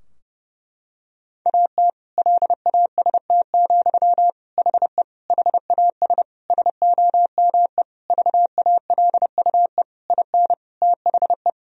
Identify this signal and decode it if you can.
Morse code, heard as a keyed tone, about 30 wpm, 700 Hz. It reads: AT LAST, HE HAS SOME VALUE IN THE
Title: Sentences from Top 1000 Words - Review Part 3 - 30wpm